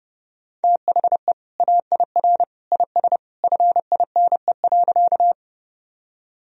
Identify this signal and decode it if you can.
Morse code, heard as a keyed tone, about 30 wpm, 700 Hz. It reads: THE AIR IS FINE.
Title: Sentences from Top 1000 Words - Review Part 1 - 30wpm